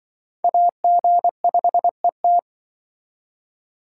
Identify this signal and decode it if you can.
Morse code, heard as a keyed tone, about 25 wpm, 700 Hz. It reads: AG5ET
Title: Speed Racing - USA 2x2 Call Signs 20wpm